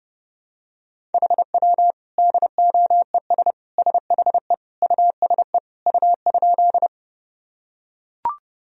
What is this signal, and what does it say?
Morse code, heard as a keyed tone, about 30 wpm, 700 Hz. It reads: HW DOES SHE USE U?
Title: Sentences from Top 200 Words and Abbreviations 30wpm